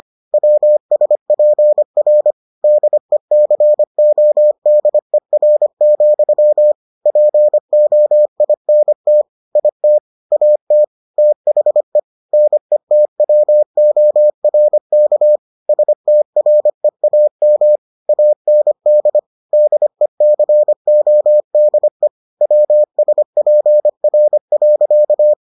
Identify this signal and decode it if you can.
Morse code, heard as a keyed tone, about 25 wpm, 600 Hz. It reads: WSPR DECODER, POINT IT AT THE NETWORK STREAM AND DECODE WSPR.